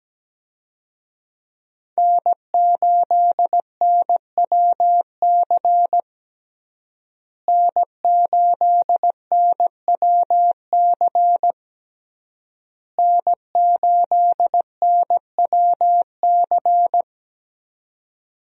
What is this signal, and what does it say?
Morse code, heard as a keyed tone, about 17 wpm, 700 Hz. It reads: N8NWC N8NWC N8NWC
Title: USA Call Signs - 5 Characters - Repeated 3x - 17wpm